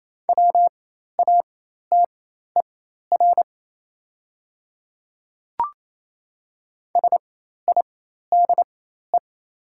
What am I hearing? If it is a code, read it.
Morse code, heard as a keyed tone, about 28 wpm, spaced as for 13 wpm, 700 Hz. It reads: WATER SIDE
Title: 4x Character Spacing - Top 100 Words - 28wpm